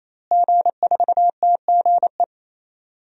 Morse code, heard as a keyed tone, about 28 wpm, 700 Hz. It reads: G4TGE
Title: DXCC-100 Call Signs - 5 Characters - Repeated 3x - 28wpm